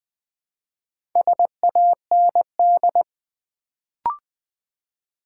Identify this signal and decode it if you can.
Morse code, heard as a keyed tone, about 20 wpm, 700 Hz. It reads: SAND
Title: Build A Word - 2 Letters At A Time - 4-Letter Words - 20wpm